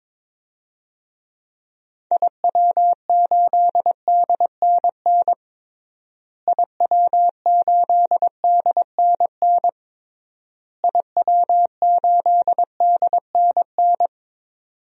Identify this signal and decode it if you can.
Morse code, heard as a keyed tone, about 22 wpm, 700 Hz. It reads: IW8DNN IW8DNN IW8DNN
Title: DXCC-100 Call Signs - 6 Characters - Repeated 3x - 22wpm